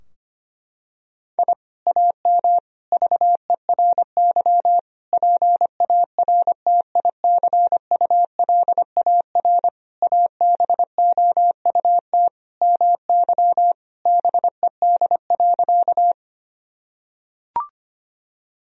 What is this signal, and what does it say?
Morse code, heard as a keyed tone, about 25 wpm, 700 Hz. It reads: I AM VERY PARTICULAR ABOUT MY BED.
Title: Sentences from Top 1000 Words - Review Part 4 - 25wpm